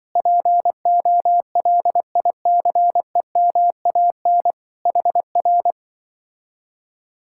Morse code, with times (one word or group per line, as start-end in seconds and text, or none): 0.2-4.5 POLICEMAN
4.9-5.7 HR